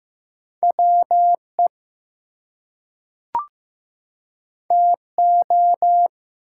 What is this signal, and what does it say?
Morse code, heard as a keyed tone, about 15 wpm, 700 Hz. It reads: WE TO
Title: Two Letter Words - 15wpm